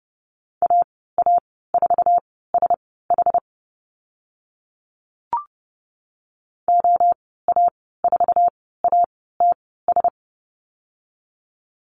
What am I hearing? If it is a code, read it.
Morse code, heard as a keyed tone, about 30 wpm, spaced as for 17 wpm, 700 Hz. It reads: AA4SH OA4ATS
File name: Lesson 14g - Number 2 - Call Signs 30wpm